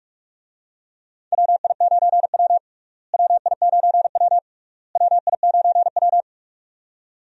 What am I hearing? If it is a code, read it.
Morse code, heard as a keyed tone, about 46 wpm, 700 Hz. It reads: WI9W WI9W WI9W
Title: USA Call Signs - 4 Characters - Repeated 3x - 45wpm